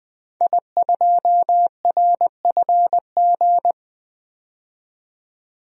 Morse code, heard as a keyed tone, about 20 wpm, 700 Hz. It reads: I2RFG
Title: DXCC-100 Call Signs - 5 Characters - 20wpm